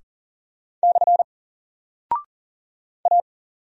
Morse code, heard as a keyed tone, about 38 wpm, 700 Hz. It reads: / A
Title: Single Character (Letter:Number:Punctuation:Prosign) 40wpm